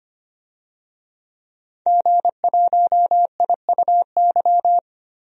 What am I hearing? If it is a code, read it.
Morse code, heard as a keyed tone, about 25 wpm, 700 Hz. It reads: G1IUY
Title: DXCC-100 Call Signs - 5 Characters - 25wpm